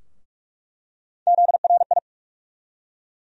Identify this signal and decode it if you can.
Morse code, heard as a keyed tone, about 46 wpm, 700 Hz. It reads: ZRI
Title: USA Call Sign Postfixes - 3 Letters 45wpm